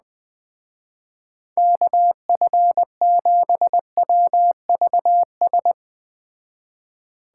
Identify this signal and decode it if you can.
Morse code, heard as a keyed tone, about 20 wpm, 700 Hz. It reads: KF7WVS
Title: Speed Racing - USA 2x3 Call Signs 17wpm